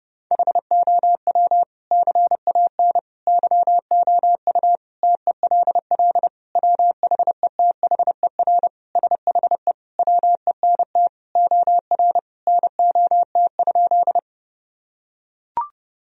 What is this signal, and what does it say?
Morse code, heard as a keyed tone, about 30 wpm, 700 Hz. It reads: HOW CAN YOU TELL WHETHER SHE WENT OR NOT?
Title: Sentences from Top 1000 Words - Review Part 3 - 30wpm